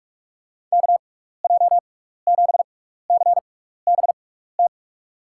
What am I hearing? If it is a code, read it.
Morse code, heard as a keyed tone, about 44 wpm, spaced as for 15 wpm, 700 Hz. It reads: KJ7CBT
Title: USA Call Signs - 6 Characters - 6x Character Spacing - 45wpm